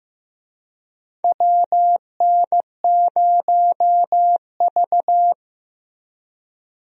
Morse code, heard as a keyed tone, about 16 wpm, 700 Hz. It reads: WN0V
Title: USA Call Signs - 4 Characters - Repeated 2x - 15wpm